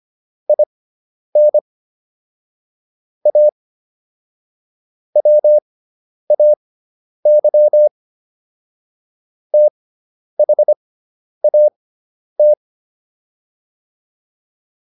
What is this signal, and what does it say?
Morse code, heard as a keyed tone, about 25 wpm, spaced as for 10 wpm, 600 Hz. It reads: IN A WAY THAT